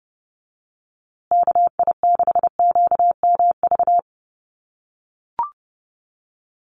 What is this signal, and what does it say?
Morse code, heard as a keyed tone, about 30 wpm, 700 Hz. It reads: KI6QMV
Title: USA Call Signs Repeated 3x 30wpm